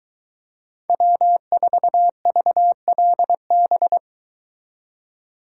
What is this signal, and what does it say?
Morse code, heard as a keyed tone, about 22 wpm, 700 Hz. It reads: W4VLB